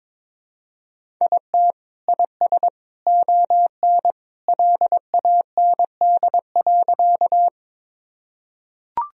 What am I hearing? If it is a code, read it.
Morse code, heard as a keyed tone, about 22 wpm, 700 Hz. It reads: IT IS ON LAND.
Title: Sentences from Top 1000 Words - Review Part 1 - 22wpm